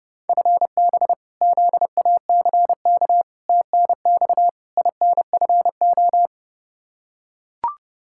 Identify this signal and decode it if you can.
Morse code, heard as a keyed tone, about 30 wpm, 700 Hz. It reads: FB ZACK TNX INFO
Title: Practice QSO - Exchange 3 - More Info - 30wpm